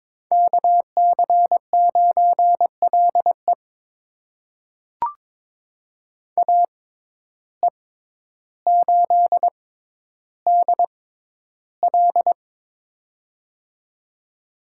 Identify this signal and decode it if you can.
Morse code, heard as a keyed tone, about 23 wpm, spaced as for 11 wpm, 700 Hz. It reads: KC9LE AE8DL
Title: USA Call Signs - 5 Characters - 6x Character Spacing - 22wpm